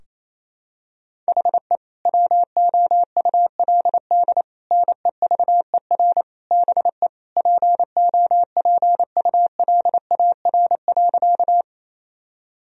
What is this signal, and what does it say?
Morse code, heard as a keyed tone, about 28 wpm, 700 Hz. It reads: HE WOULD NEVER BE POPULAR.
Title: Sentences from Top 1000 Words - Review Part 3 - 28wpm